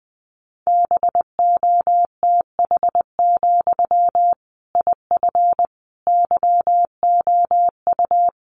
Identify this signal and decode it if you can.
Morse code, heard as a keyed tone, about 19 wpm, 700 Hz. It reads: BOTH, IF YOU
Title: Sentences from Top 1500 Words 20wpm